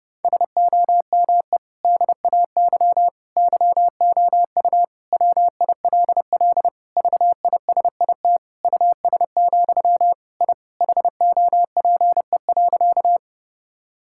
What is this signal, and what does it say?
Morse code, heard as a keyed tone, about 32 wpm, 700 Hz. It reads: SOME DAY YOU WILL VISIT US, I HOPE.
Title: Sentences from Top 1000 Words - Review Part 6 - 30wpm